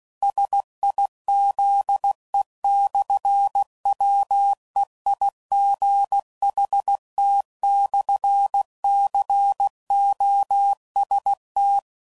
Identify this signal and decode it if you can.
Morse code, heard as a keyed tone, about 16 wpm, 800 Hz. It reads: SIZE/WEIGHT/COST
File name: CW Code Groups @15WPM/Technical Subjects